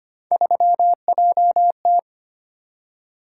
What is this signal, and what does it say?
Morse code, heard as a keyed tone, about 25 wpm, 700 Hz. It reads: 3JT
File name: Three Letter-Number Combinations - 25wpm